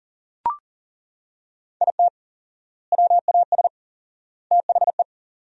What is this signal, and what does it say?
Morse code, heard as a keyed tone, about 40 wpm, 700 Hz. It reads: IT WAS THE
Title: 4x Word-Spacing - Sentences from Top 500 Words - 40wpm